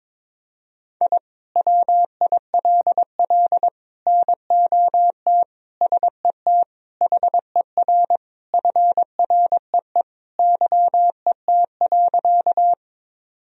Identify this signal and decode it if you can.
Morse code, heard as a keyed tone, about 22 wpm, 700 Hz. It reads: I WILL NOT SET HER FREE YET.
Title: Sentences from Top 1000 Words - Review Part 1 - 22wpm